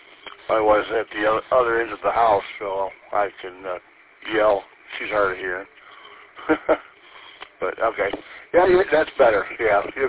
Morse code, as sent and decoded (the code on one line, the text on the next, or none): none